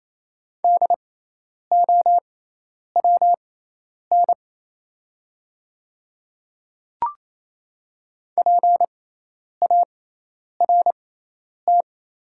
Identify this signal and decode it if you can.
Morse code, heard as a keyed tone, about 28 wpm, spaced as for 10 wpm, 700 Hz. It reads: DOWN PART
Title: 6x Character Spacing - Top 200 Words - 28wpm